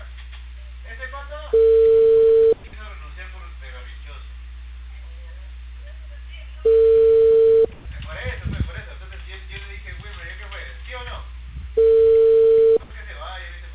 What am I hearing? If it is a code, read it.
Ringback tone (China, 450 Hz), with speech.